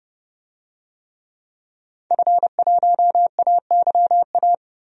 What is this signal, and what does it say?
Morse code, heard as a keyed tone, about 30 wpm, 700 Hz. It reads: F1AYA
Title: DXCC-100 Call Signs - 5 Characters - Repeated 2x - 30wpm